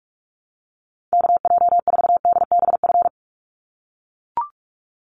Morse code, heard as a keyed tone, about 44 wpm, 700 Hz. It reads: KJ4DBF
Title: USA Call Signs 45wpm